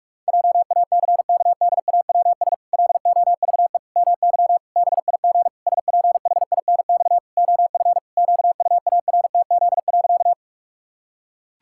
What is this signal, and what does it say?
Morse code, heard as a keyed tone, about 46 wpm, 700 Hz. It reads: JACKDAWS LOVE MY BIG SPHINX OF QUARTZ.
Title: Pangram Challenge v2 - 45wpm